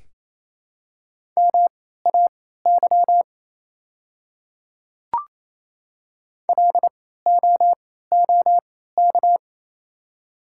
Morse code, heard as a keyed tone, about 28 wpm, spaced as for 16 wpm, 700 Hz. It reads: MAY LOOK
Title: 3x Character Spacing - Top 100 Words - 28wpm